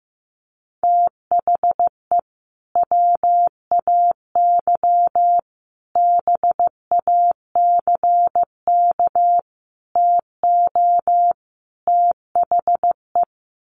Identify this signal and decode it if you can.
Morse code, heard as a keyed tone, about 15 wpm, 700 Hz. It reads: THE WAY BACK TO THE
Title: Sets of 5 Words Encore2 15wpm